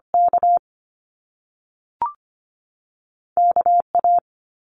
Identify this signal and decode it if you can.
Morse code, heard as a keyed tone, about 25 wpm, 700 Hz. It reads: K K A
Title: Top Two Letter Combinations in Words 25wpm